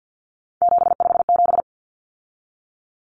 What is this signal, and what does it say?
Morse code, heard as a keyed tone, about 50 wpm, 700 Hz. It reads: 757
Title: Common Cut Numbers - 3-Digit 50wpm